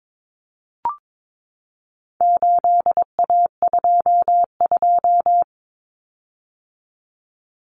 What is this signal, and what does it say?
Morse code, heard as a keyed tone, about 22 wpm, 700 Hz. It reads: 8A22
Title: Common Cut Numbers - 4-Digit 22wpm